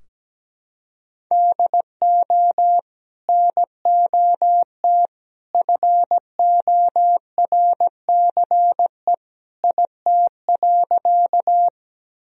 Morse code, heard as a keyed tone, about 17 wpm, 700 Hz. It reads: DO NOT FORCE IT.
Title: Sentences from Top 1000 Words - Review Part 3 - 17wpm